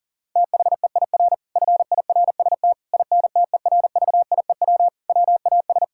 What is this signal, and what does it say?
Morse code, heard as a keyed tone, about 40 wpm, 700 Hz. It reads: THEIR FIRST INTERVIEW WAS
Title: Sentences from Top 1600 Words 40wpm